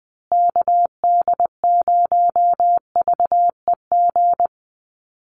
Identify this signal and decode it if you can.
Morse code, heard as a keyed tone, about 20 wpm, 700 Hz. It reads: KD0VEG
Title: USA Call Signs Repeated 2x 20wpm